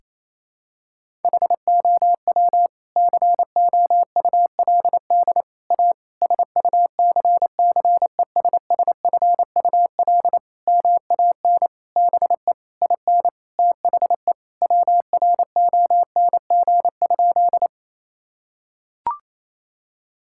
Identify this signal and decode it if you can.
Morse code, heard as a keyed tone, about 28 wpm, 700 Hz. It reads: HOW COULD A SUCCESSFUL MAN BE IN THE WRONG?
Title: Sentences from Top 1000 Words - Review Part 6 - 28wpm